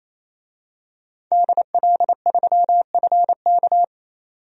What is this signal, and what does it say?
Morse code, heard as a keyed tone, about 28 wpm, 700 Hz. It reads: DL3FK
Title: DXCC-100 Call Signs - 5 Characters - 28wpm